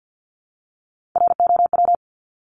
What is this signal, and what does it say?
Morse code, heard as a keyed tone, about 50 wpm, 700 Hz. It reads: ROW
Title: Top 1751-2000 Words 50wpm